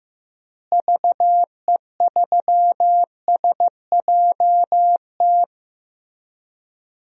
Morse code, heard as a keyed tone, about 16 wpm, 700 Hz. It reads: VE3SJT
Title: DXCC-100 Call Signs - 6 Characters - Repeated 3x - 15wpm